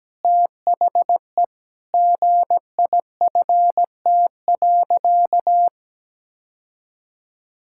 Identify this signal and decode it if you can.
Morse code, heard as a keyed tone, about 17 wpm, 700 Hz. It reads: THE GIFT.
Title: Sentences from Top 1000 Words - Review Part 6 - 17wpm